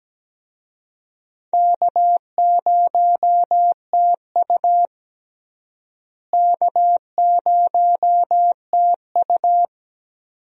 Morse code, heard as a keyed tone, about 17 wpm, 700 Hz. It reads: K0TU K0TU
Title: USA Call Signs - 4 Characters - Repeated 2x - 17wpm